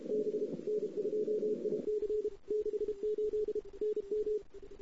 Morse code, heard as a keyed tone, about 31 wpm, 410 Hz. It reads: B N2IC 68 NM
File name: CW / 2008-ss-cw-k5tr-op-n5rz -08:00